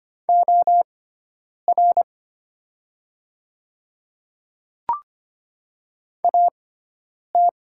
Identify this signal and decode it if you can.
Morse code, heard as a keyed tone, about 25 wpm, spaced as for 9 wpm, 700 Hz. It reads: OR AT
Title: 6x Character Spacing - Top 100 Words - 25wpm